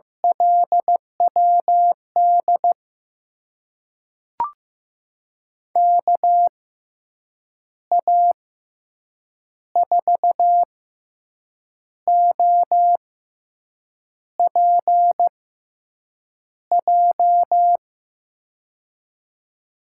Morse code, heard as a keyed tone, about 15 wpm, spaced as for 6 wpm, 700 Hz. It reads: LWD KA4OPJ
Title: USA Call Signs - 6 Characters - 6x Character Spacing - 15wpm